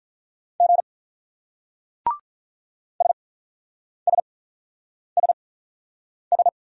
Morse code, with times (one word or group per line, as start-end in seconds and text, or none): 0.6-0.8 G
3.0-6.5 SSSS